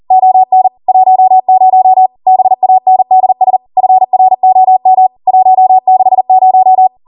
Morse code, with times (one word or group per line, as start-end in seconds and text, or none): 0.1-0.7 ON
0.9-2.1 10
2.3-3.6 BANDS
3.8-5.1 FROM
5.3-6.9 160